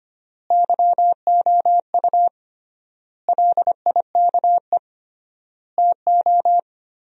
Morse code, heard as a keyed tone, about 25 wpm, 700 Hz. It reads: YOU LIKE TO
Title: 3x Word-Spacing - Sentences from Top 100 Words - 25wpm